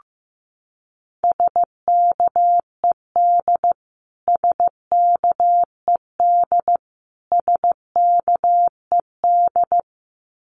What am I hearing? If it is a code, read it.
Morse code, heard as a keyed tone, about 15 wpm, 700 Hz. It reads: SKED SKED SKED
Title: Abbreviations Repeated 3x - 15wpm